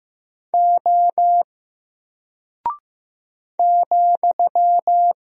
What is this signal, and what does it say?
Morse code, heard as a keyed tone, about 15 wpm, 700 Hz. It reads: O ,
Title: Single Character (Letter:Number:Punctuation:Prosign) Alt 15wpm